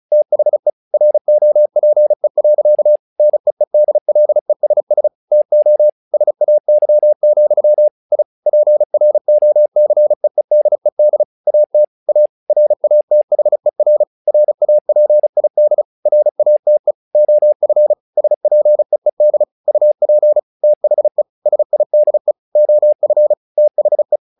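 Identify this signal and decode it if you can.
Morse code, heard as a keyed tone, about 34 wpm, 600 Hz. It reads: THE ROPE. NEEDLESS TO SAY, I PROCEEDED AT A RATHER RAPID RATE OF SPEED UP THE SIDE OF THE
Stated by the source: Morse practice text "Tower Trouble" sent at 35 WPM, from the SKCC - Straight Key Century Club learning center audio files.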